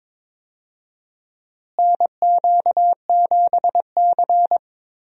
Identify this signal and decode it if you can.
Morse code, heard as a keyed tone, about 22 wpm, 700 Hz. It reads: NQ7C
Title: USA Call Signs - 4 Characters - Repeated 3x - 22wpm